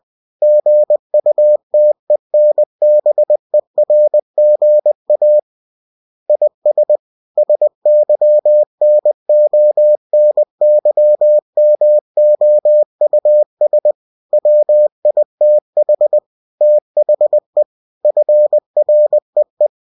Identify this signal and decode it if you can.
Morse code, heard as a keyed tone, about 20 wpm, 600 Hz. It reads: GUTENBERGA IS SYNONYMOUS WITH THE FREE
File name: cw-books / winnie-the-pooh / winnie-20wpm-page-0120